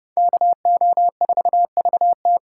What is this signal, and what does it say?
Morse code, heard as a keyed tone, about 30 wpm, 700 Hz. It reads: KO4VT